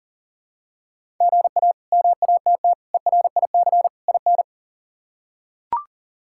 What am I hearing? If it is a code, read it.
Morse code, heard as a keyed tone, about 40 wpm, 700 Hz. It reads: GA MATT ERIC IN
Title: K1USN SST QSO Practice - 40wpm